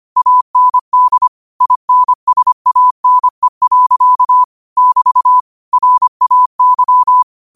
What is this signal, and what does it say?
Morse code, heard as a keyed tone, about 25 wpm, 1000 Hz. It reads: AND INSANE. <BT> RAY
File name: qotdfiles / 25wpm